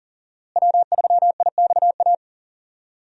Morse code, heard as a keyed tone, about 40 wpm, 700 Hz. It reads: W3IXA
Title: USA Call Signs - 5 Characters - Repeated 3x - 40wpm